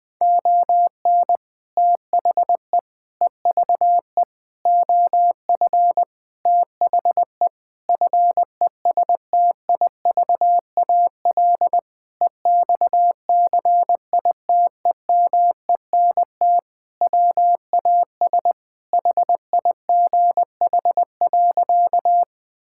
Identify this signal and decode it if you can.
Morse code, heard as a keyed tone, about 20 wpm, 700 Hz. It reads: ON THE EVE OF THE FESTIVAL EXCITEMENT WAS HIGH.
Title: Sentences with 5-Word Phrases - Spoken Beginning - 20wpm